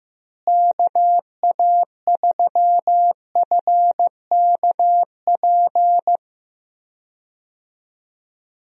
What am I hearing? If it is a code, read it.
Morse code, heard as a keyed tone, about 15 wpm, 700 Hz. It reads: KA3FKP
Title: USA Call Signs - 6 Characters - Repeated 3x - 15wpm